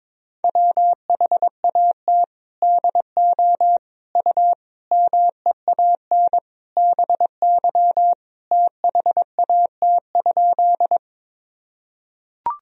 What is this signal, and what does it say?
Morse code, heard as a keyed tone, about 22 wpm, 700 Hz. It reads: WHAT DO U MEAN BY THAT?
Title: Sentences from Top 200 Words and Abbreviations 22wpm